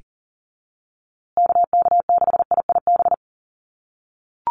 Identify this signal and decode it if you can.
Morse code, heard as a keyed tone, about 40 wpm, 700 Hz. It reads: KK6IIB
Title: USA Call Signs Repeated 2x 40wpm